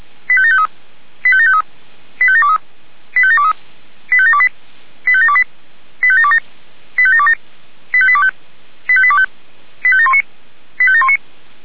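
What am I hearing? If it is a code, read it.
Five-tone selective call (ZVEI), heard as quick runs of five tones, tones of about 70 ms. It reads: 85753 85753 85723 85723 85728 85728 85727 85727 85825 85825 85719 85719
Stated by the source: Five-tone ZVEI selective call received on an SDR.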